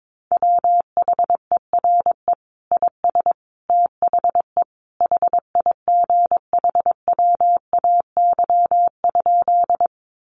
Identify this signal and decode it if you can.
Morse code, heard as a keyed tone, about 22 wpm, 700 Hz. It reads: WHERE IS THE HIGHWAY?
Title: Sentences from Top 1400 Words 22wpm